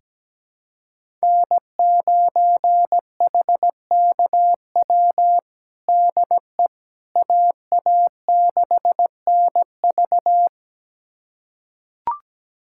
Morse code, heard as a keyed tone, about 17 wpm, 700 Hz. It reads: N9HKW DE AA6NV
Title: USA Callsign DE Callsign - 5 Characters - 17wpm